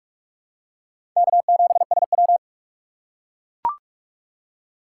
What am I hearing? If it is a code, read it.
Morse code, heard as a keyed tone, about 43 wpm, 700 Hz. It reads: K7SW
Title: USA Call Signs - 4 Characters - 6x Character Spacing - 45wpm